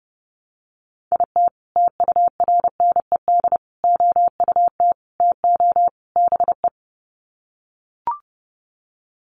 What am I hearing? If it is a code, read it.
Morse code, heard as a keyed tone, about 30 wpm, 700 Hz. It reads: IT TURNED OUT TO BE
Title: Sets of 5 Words 30wpm